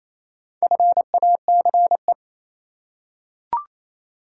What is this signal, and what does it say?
Morse code, heard as a keyed tone, about 27 wpm, 700 Hz. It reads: FACE
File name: Easy Words - Speed-Racing with Hints - 22wpm